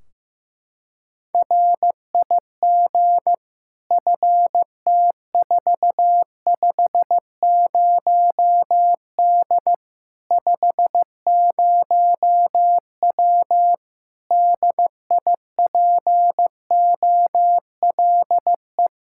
Morse code, heard as a keyed tone, about 15 wpm, 700 Hz. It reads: RIG FT450D 50W DIPOLE
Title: Practice QSO - Exchange 2 - Info - 15wpm